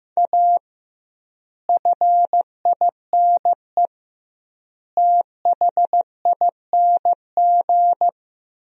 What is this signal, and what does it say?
Morse code, heard as a keyed tone, about 15 wpm, 700 Hz. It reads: A FINE THING
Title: 2x Word-Spacing - Sentences from Top 500 Words - 15wpm